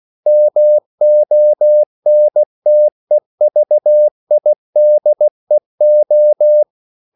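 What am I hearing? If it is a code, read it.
Morse code, heard as a keyed tone, about 16 wpm, 600 Hz. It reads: MONTEVIDEO
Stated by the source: CW practice of national capitals at 16 words per minute, from ON6ZQ.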